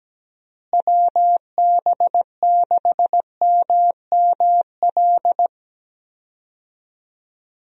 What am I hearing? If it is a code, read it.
Morse code, heard as a keyed tone, about 17 wpm, 700 Hz. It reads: WB6MML
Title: USA Call Signs - 6 Characters - Repeated 3x - 17wpm